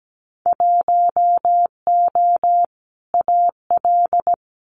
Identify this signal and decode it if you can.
Morse code, heard as a keyed tone, about 17 wpm, 700 Hz. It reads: 1O AL
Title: Winter Field Day Exchanges 17wpm